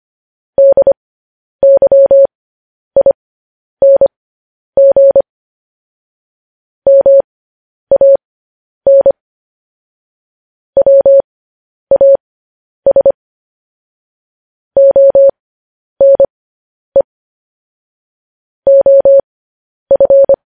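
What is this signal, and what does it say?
Morse code, heard as a keyed tone, about 25 wpm, spaced as for 10 wpm, 550 Hz. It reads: DYING MAN WAS ONE OF
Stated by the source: Morse code transmission of news headlines.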